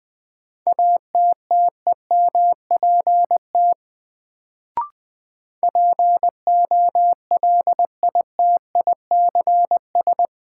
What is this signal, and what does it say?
Morse code, heard as a keyed tone, about 20 wpm, 700 Hz. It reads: ATTEMPT POLITICS
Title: Sentences from Top 1000 Words - New Word Drill - 20wpm